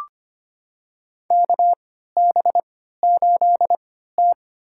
Morse code, heard as a keyed tone, about 25 wpm, spaced as for 14 wpm, 700 Hz. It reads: KB8T
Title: USA Call Signs - 4 Characters - 3x Character Spacing - 25wpm